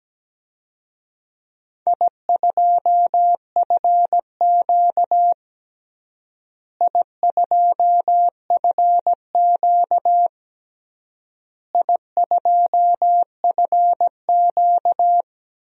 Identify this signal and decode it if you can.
Morse code, heard as a keyed tone, about 17 wpm, 700 Hz. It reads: I2FQ I2FQ I2FQ
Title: DXCC-100 Call Signs - 4 Characters - Repeated 3x - 17wpm